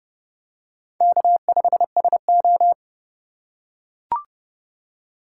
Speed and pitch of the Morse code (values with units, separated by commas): 31 wpm, 700 Hz